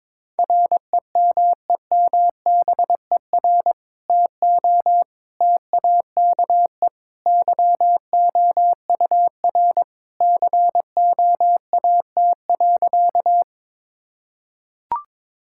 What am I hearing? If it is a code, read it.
Morse code, heard as a keyed tone, about 22 wpm, 700 Hz. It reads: REMEMBER TO TAKE YOUR COAT.
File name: practice / 22wpm / Sentences with 2-Word Phrases - 22wpm